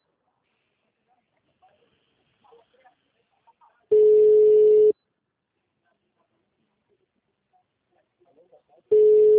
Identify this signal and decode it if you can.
Ringback tone (Europe, 425 Hz), with speech.